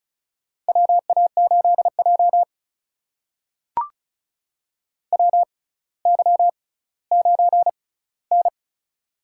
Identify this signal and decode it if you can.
Morse code, heard as a keyed tone, about 36 wpm, spaced as for 18 wpm, 700 Hz. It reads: WA8J WY9N
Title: USA Call Signs - 6x Character Spacing - 35wpm